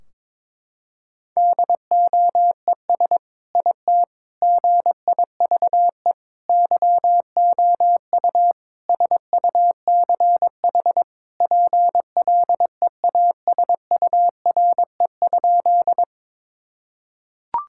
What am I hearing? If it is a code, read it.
Morse code, heard as a keyed tone, about 22 wpm, 700 Hz. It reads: DOES IT GIVE YOU SUCH PLEASURE?
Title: Sentences from Top 1600 Words 22wpm